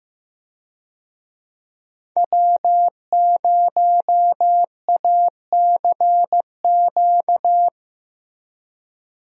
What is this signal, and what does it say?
Morse code, heard as a keyed tone, about 16 wpm, 700 Hz. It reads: W0ACQ
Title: USA Call Signs - 5 Characters - Repeated 3x - 15wpm